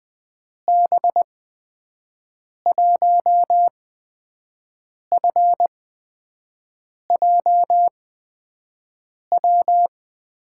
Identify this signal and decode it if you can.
Morse code, heard as a keyed tone, about 20 wpm, spaced as for 5 wpm, 700 Hz. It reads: B1FJW
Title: USA Call Signs - 6 Characters - 8x Character Spacing - 20wpm